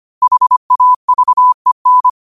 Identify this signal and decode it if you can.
Morse code, heard as a keyed tone, about 25 wpm, 1000 Hz. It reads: HAVEN
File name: qotdfiles / 25wpm